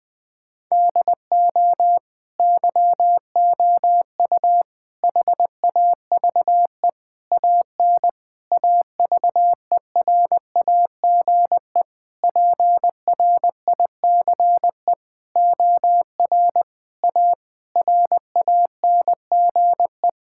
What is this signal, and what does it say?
Morse code, heard as a keyed tone, about 20 wpm, 700 Hz. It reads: DO YOU HAVE AN AVERAGE PRICE OR A RANGE
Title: Sentences with 9-Word Phrases - Spoken Beginning - 20wpm